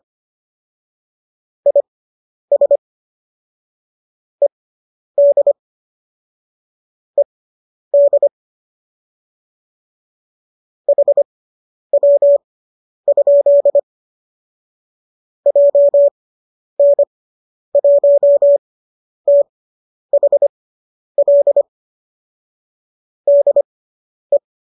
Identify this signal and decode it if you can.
Morse code, heard as a keyed tone, about 25 wpm, spaced as for 10 wpm, 570 Hz. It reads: IS ED ED HW? JN1THL DE